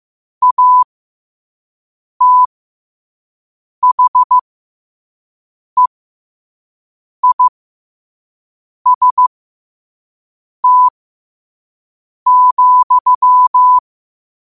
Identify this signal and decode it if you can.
Morse code, heard as a keyed tone, about 15 wpm, spaced as for 5 wpm, 1000 Hz. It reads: ATHEIST,
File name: qotdfiles / 5wpm